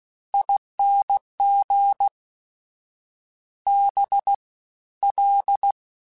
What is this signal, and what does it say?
Morse code, heard as a keyed tone, about 16 wpm, spaced as for 13 wpm, 800 Hz. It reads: ING BL
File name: Special INGs-mp3s / Special INGs-mp3s / Fast ing @15